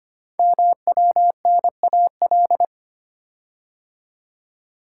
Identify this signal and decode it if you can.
Morse code, heard as a keyed tone, about 25 wpm, 700 Hz. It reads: MWNAL
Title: Code Groups - 5 Letters - 25wpm